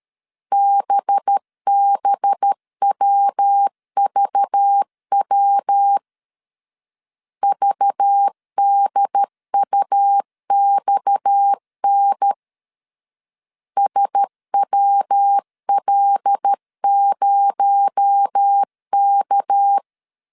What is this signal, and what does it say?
Morse code, heard as a keyed tone, about 13 wpm, 800 Hz. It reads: BBWVW VDUXN SWL0K
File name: CW / DET / 1 6 1500C S A 12wpm 3440